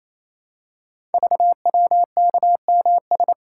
Morse code, heard as a keyed tone, about 28 wpm, 700 Hz. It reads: VWKMS
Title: Code Groups - 5 Letters - 28wpm